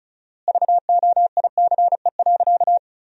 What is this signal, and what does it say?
Morse code, heard as a keyed tone, about 36 wpm, 700 Hz. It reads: VOICE.
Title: 4x Word-Spacing - Sentences from Top 500 Words - 35wpm